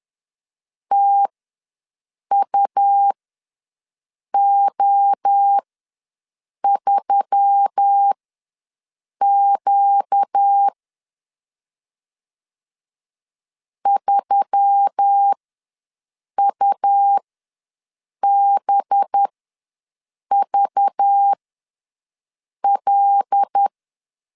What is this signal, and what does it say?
Morse code, heard as a keyed tone, about 10 wpm, spaced as for 6 wpm, 800 Hz. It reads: TUO3Q 3UBVL
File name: LEZIONI / 5 LBVC / 5 LBV ws 10wpm 3645